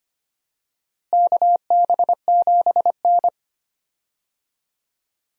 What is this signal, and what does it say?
Morse code, heard as a keyed tone, about 25 wpm, 700 Hz. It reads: KB7N
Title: USA Call Signs - 4 Characters - Repeated 3x - 25wpm